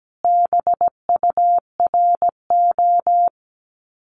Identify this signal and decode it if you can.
Morse code, heard as a keyed tone, about 17 wpm, 700 Hz. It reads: BURO
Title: Common QSO Phrases 17wpm